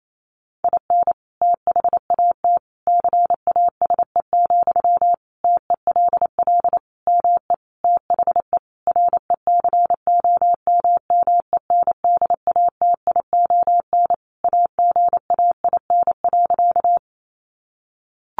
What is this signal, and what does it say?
Morse code, heard as a keyed tone, about 28 wpm, 700 Hz. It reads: IN THAT CASE, TELL ME THE RECOMMENDATION AGAIN.
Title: Sentences from Top 1400 Words 28wpm